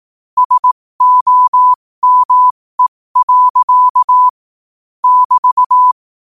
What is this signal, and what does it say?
Morse code, heard as a keyed tone, about 18 wpm, spaced as for 15 wpm, 1000 Hz. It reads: SOME. <BT>
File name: 15wpm